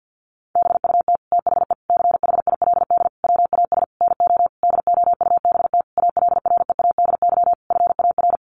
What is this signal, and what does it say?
Morse code, heard as a keyed tone, about 50 wpm, 700 Hz. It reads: BUT THE CHILD WAS NO DOUBT ALREADY FAR
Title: Sentences from Top 1500 Words 50wpm